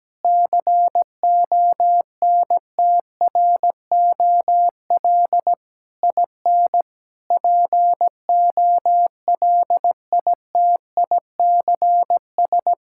Morse code, heard as a keyed tone, about 17 wpm, 700 Hz. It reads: CONTROL IN POLITICS
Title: Sentences from Top 1000 Words - Review Part 6 - 17wpm